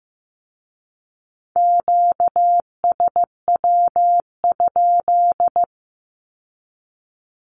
Morse code, heard as a keyed tone, about 15 wpm, 700 Hz. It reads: QSW?
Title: Q-Codes with Spoken Meaning - 15wpm